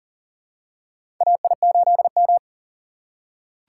Morse code, heard as a keyed tone, about 40 wpm, 700 Hz. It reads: AI8M
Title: USA Call Signs - 4 Characters - 3x Character Spacing - 40wpm